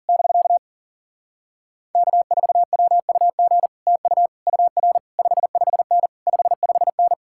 Morse code, heard as a keyed tone, about 47 wpm, 700 Hz. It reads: <BK> K4WUG TU UR 55N 55N
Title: Speed-Racing Common QSO Phrases - 40wpm